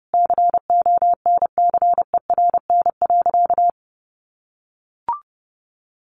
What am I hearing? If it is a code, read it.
Morse code, heard as a keyed tone, about 30 wpm, 700 Hz. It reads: CONCERN.
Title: Sentences from Top 1400 Words 30wpm